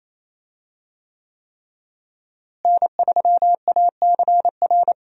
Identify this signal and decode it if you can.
Morse code, heard as a keyed tone, about 28 wpm, 700 Hz. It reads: N3ACR
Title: USA Call Signs - 5 Characters - Repeated 3x - 28wpm